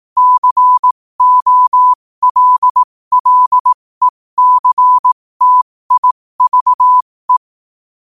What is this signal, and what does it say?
Morse code, heard as a keyed tone, about 18 wpm, spaced as for 15 wpm, 1000 Hz. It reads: COLLECTIVE
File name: qotdfiles / 15wpm